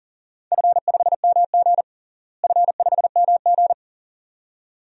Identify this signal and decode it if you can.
Morse code, heard as a keyed tone, about 40 wpm, 700 Hz. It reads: F5MG F5MG
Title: DXCC-100 Call Signs - 4 Characters - Repeated 2x - 40wpm